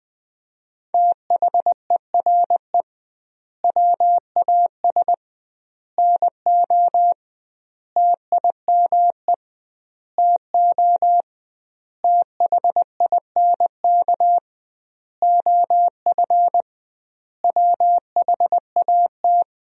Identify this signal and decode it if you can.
Morse code, heard as a keyed tone, about 20 wpm, 700 Hz. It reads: THERE WAS NO TIME TO THINK OF WHAT
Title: 2x Word-Spacing - Sentences from Top 400 Words - 20wpm